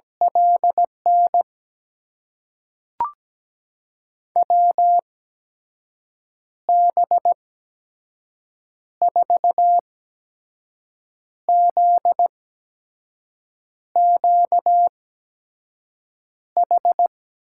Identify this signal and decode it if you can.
Morse code, heard as a keyed tone, about 17 wpm, spaced as for 5 wpm, 700 Hz. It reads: LN WB4ZQH
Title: USA Call Signs - 8x Character Spacing - 17wpm